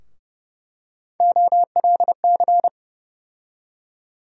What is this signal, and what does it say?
Morse code, heard as a keyed tone, about 32 wpm, 700 Hz. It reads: OLC
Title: USA Call Sign Postfixes - 3 Letters 30wpm